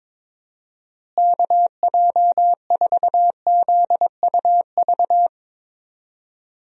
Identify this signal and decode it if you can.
Morse code, heard as a keyed tone, about 22 wpm, 700 Hz. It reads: KJ4ZUV
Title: USA Call Signs - 6 Characters - Repeated 2x - 22wpm